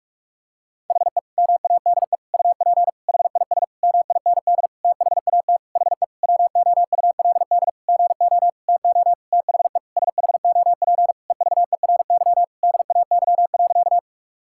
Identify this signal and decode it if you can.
Morse code, heard as a keyed tone, about 47 wpm, 700 Hz. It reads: HE MADE UP HIS MIND THAT HE WOULD GO TO THE SHOP EVERY DAY.
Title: Sentences from Top 1600 Words 45wpm